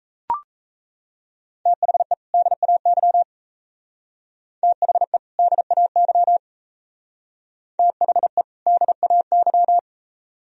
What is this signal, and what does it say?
Morse code, heard as a keyed tone, about 36 wpm, 700 Hz. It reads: THE DAY THE DAY THE DAY